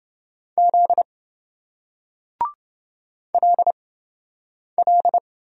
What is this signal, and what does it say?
Morse code, heard as a keyed tone, about 28 wpm, spaced as for 7 wpm, 700 Hz. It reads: Z LL